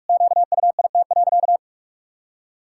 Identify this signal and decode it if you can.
Morse code, heard as a keyed tone, about 46 wpm, 700 Hz. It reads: QUIT.
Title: Pangram Challenge v2 - 45wpm